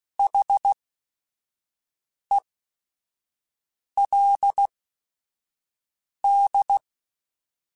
Morse code, heard as a keyed tone, about 16 wpm, spaced as for 5 wpm, 800 Hz. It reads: HELD
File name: QSOs / QSOs-mp3s / kc7sac@5